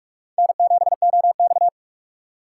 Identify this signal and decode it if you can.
Morse code, heard as a keyed tone, about 44 wpm, 700 Hz. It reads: N7OX
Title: USA Call Signs - 6x Character Spacing - 45wpm